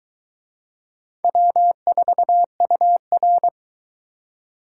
Morse code, heard as a keyed tone, about 22 wpm, 700 Hz. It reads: W4UR